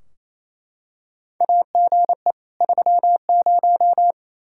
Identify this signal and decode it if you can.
Morse code, heard as a keyed tone, about 28 wpm, 700 Hz. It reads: AGE 30
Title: Practice QSO - Exchange 3 - More Info - 28wpm